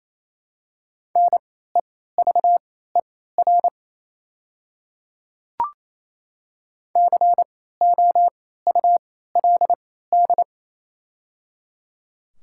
Morse code, heard as a keyed tone, about 28 wpm, spaced as for 16 wpm, 700 Hz. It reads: NEVER COULD
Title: 3x Character Spacing - Top 300 Words - 28wpm